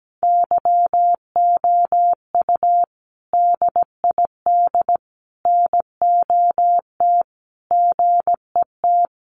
Morse code, heard as a keyed tone, about 17 wpm, 700 Hz. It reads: YOU DID NOT GET
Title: Sentences from Top 600 Words 17wpm